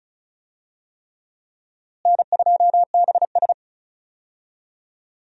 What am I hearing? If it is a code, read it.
Morse code, heard as a keyed tone, about 36 wpm, 700 Hz. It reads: N2BS
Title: USA Call Signs - 4 Characters - Repeated 3x - 35wpm